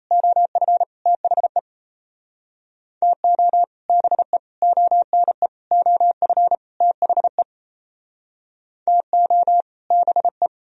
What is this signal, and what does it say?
Morse code, heard as a keyed tone, about 38 wpm, 700 Hz. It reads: OF THE TO BE ONE OF THE TO BE